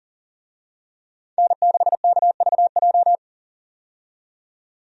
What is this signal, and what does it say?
Morse code, heard as a keyed tone, about 40 wpm, 700 Hz. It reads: N6KVJ
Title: USA Call Signs - 5 Characters - 40wpm